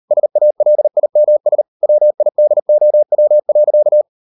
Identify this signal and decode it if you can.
Morse code, heard as a keyed tone, about 38 wpm, 600 Hz. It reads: SALIMS WIDOW.